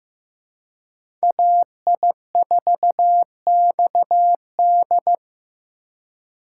Morse code, heard as a keyed tone, about 18 wpm, 700 Hz. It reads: AI4XD
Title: Speed Racing - USA Call Signs 15wpm